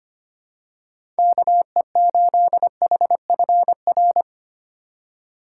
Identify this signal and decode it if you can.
Morse code, heard as a keyed tone, about 25 wpm, 700 Hz. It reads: KE8HFR
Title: USA Call Signs - 6 Characters - 4x Character Spacing - 25wpm